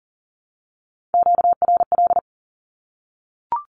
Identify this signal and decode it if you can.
Morse code, heard as a keyed tone, about 40 wpm, 700 Hz. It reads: QRL
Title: Q-Codes Repeated 3x 40wpm